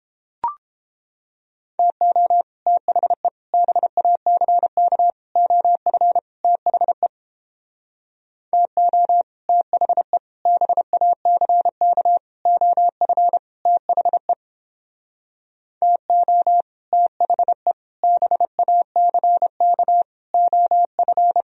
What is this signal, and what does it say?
Morse code, heard as a keyed tone, about 29 wpm, 700 Hz. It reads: TO THE BACK OF THE TO THE BACK OF THE TO THE BACK OF